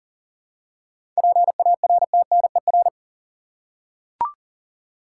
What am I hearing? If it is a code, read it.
Morse code, heard as a keyed tone, about 40 wpm, 700 Hz. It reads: PARTNER
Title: Seven Letter Words - 40wpm